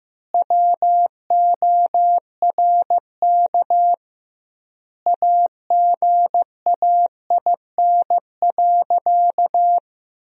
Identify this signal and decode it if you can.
Morse code, heard as a keyed tone, about 15 wpm, 700 Hz. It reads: WORK AGAIN.
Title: 2x Word-Spacing - Sentences from Top 200 Words - 15wpm